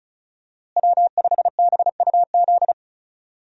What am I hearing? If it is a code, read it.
Morse code, heard as a keyed tone, about 36 wpm, 700 Hz. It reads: W5BUZ
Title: USA Call Signs - 5 Characters - Repeated 2x - 35wpm